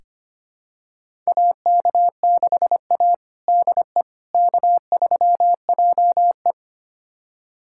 Morse code, heard as a keyed tone, about 25 wpm, 700 Hz. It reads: AK6A DE K3JE
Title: USA Callsign DE Callsign - 4 Characters - 25wpm